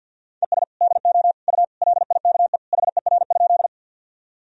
Morse code, heard as a keyed tone, about 51 wpm, 700 Hz. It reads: ES DO U LIKE HER?
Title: Sentences from Top 100 Words and Abbreviations 50wpm